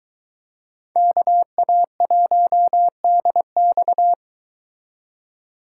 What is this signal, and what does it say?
Morse code, heard as a keyed tone, about 22 wpm, 700 Hz. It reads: KA1DX